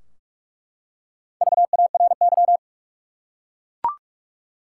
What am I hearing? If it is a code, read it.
Morse code, heard as a keyed tone, about 46 wpm, 700 Hz. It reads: VARY
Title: Four Letter Words - 45wpm